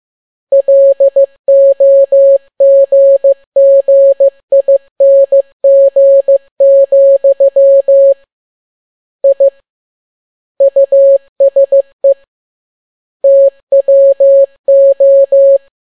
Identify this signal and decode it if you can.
Morse code, heard as a keyed tone, about 15 wpm, 550 Hz. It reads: LOGGING, I USE TWO